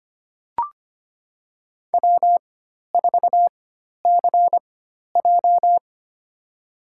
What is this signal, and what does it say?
Morse code, heard as a keyed tone, about 25 wpm, spaced as for 12 wpm, 700 Hz. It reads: W4CJ
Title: USA Call Signs - 4 Characters - 4x Character Spacing - 25wpm